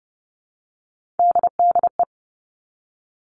Morse code, heard as a keyed tone, about 30 wpm, 700 Hz. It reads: DDE
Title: All Cut Numbers - 3-Digit 30wpm